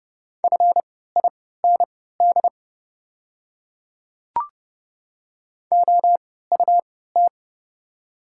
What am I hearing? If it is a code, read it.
Morse code, heard as a keyed tone, about 32 wpm, spaced as for 17 wpm, 690 Hz. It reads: FIND OUT
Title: 3x Character Spacing - Top 100 Words - 30wpm